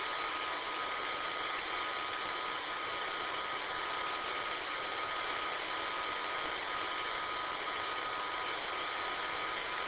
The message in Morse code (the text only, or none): none